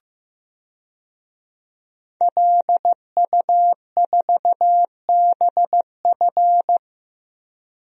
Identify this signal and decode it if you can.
Morse code, heard as a keyed tone, about 15 wpm, 700 Hz. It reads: LU4BF
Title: DXCC-100 Call Signs - 5 Characters - 15wpm